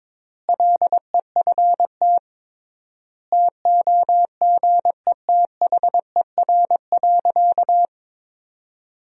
Morse code, heard as a keyed tone, about 21 wpm, 700 Hz. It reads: LEFT TOGETHER.
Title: 3x Word-Spacing - Sentences from Top 400 Words - 22wpm